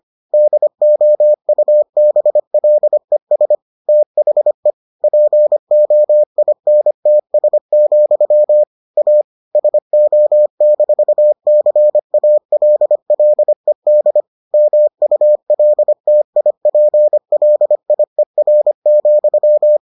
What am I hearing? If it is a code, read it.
Morse code, heard as a keyed tone, about 25 wpm, 600 Hz. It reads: DOUBLES THE POINTS, A SO-CALLED MULTIPLIER,